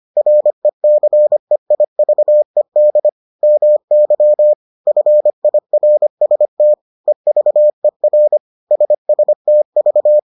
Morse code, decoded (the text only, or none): RECEIVED MY FIRST EVER SSTV